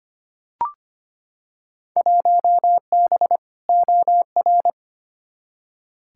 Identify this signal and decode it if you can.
Morse code, heard as a keyed tone, about 25 wpm, 700 Hz. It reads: 1B OR
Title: Field Day Exchanges - 25wpm